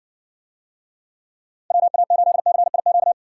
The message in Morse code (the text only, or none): WA8CIQ